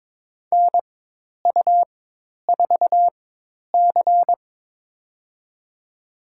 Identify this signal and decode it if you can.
Morse code, heard as a keyed tone, about 22 wpm, spaced as for 10 wpm, 700 Hz. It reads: NU4C
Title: USA Call Signs - 4 Characters - 4x Character Spacing - 22wpm